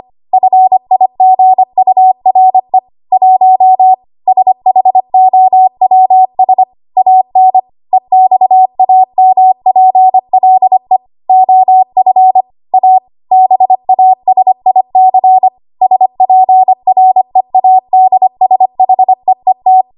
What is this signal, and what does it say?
Morse code, heard as a keyed tone, about 25 wpm, 750 Hz. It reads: FIGURE 1 SHOWS AN EXAMPLE OF A BASIC SPREADSHEET
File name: Morse / 25WPM